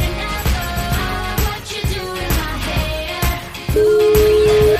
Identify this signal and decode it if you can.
Ringback tone (Europe, 425 Hz), with noise.